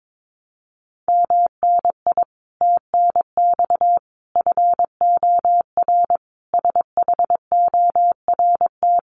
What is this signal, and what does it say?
Morse code, heard as a keyed tone, about 22 wpm, 700 Hz. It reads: MNI TNX FOR SHORT
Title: Common QSO Phrases 22wpm